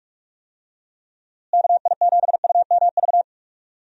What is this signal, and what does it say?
Morse code, heard as a keyed tone, about 45 wpm, 700 Hz. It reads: KI7UMV
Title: USA Call Signs - 6 Characters - Repeated 3x - 45wpm